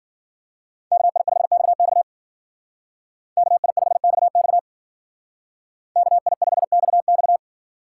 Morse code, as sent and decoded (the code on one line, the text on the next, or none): -.- .. ..... -..- -..- / -.- .. ..... -..- -..- / -.- .. ..... -..- -..-
KI5XX KI5XX KI5XX